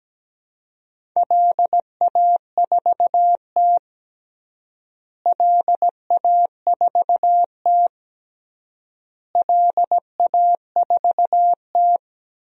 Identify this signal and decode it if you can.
Morse code, heard as a keyed tone, about 17 wpm, 700 Hz. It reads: LA4T LA4T LA4T
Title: DXCC-100 Call Signs - 4 Characters - Repeated 3x - 17wpm